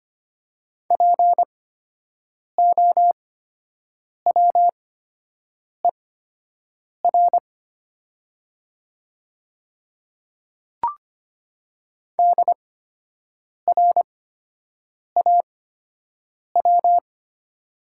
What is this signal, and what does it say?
Morse code, heard as a keyed tone, about 25 wpm, spaced as for 7 wpm, 700 Hz. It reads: POWER DRAW
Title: 8x Character Spacing - Top 500 Words - 25wpm